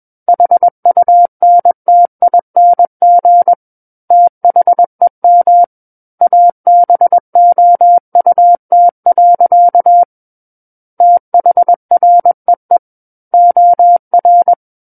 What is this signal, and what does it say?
Morse code, heard as a keyed tone, about 21 wpm, 700 Hz. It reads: HUNTING THEM ABOUT. THREE OR
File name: Book One / 12-Chapter 12 - 21 WPM